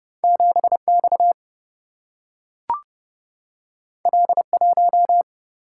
Morse code, heard as a keyed tone, about 30 wpm, 700 Hz. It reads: 7 X L 1
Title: Single-Letter Single-Number 30wpm